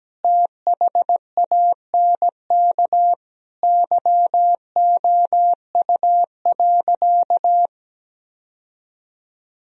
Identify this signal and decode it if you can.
Morse code, heard as a keyed tone, about 17 wpm, 700 Hz. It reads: THANK YOU.
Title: Sentences from Top 1000 Words - Review Part 1 - 17wpm